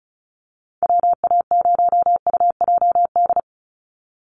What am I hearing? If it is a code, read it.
Morse code, heard as a keyed tone, about 34 wpm, 700 Hz. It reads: WA0UJD
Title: USA Call Signs 35wpm